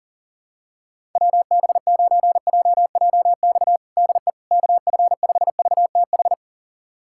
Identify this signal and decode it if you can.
Morse code, heard as a keyed tone, about 40 wpm, 700 Hz. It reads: WB9JJX DE KF5VTH
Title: USA Callsign DE Callsign - 6 Characters - 40wpm